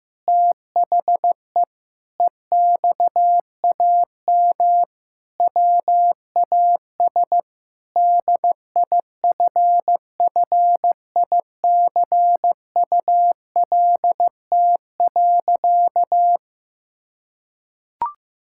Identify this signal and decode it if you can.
Morse code, heard as a keyed tone, about 15 wpm, 700 Hz. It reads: THE EXAM WAS DIFFICULT.
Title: Sentences from Top 1000 Words - Review Part 5 - 15wpm